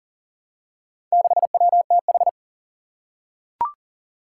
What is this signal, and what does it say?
Morse code, heard as a keyed tone, about 37 wpm, 700 Hz. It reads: 6WTH
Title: Four Challenging Letter-Number Combinations 40wpm